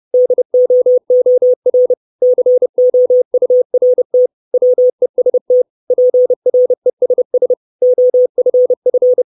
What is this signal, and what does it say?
Morse code, heard as a keyed tone, about 30 wpm, 500 Hz. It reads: DOOR COURT WEST PRESS OFF
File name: 632 Words 30WPM